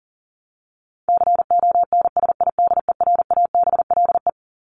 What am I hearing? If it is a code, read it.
Morse code, heard as a keyed tone, about 40 wpm, 700 Hz. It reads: CONSIDERABLE
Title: LY Suffix - Top 100 Words with Base Words 40wpm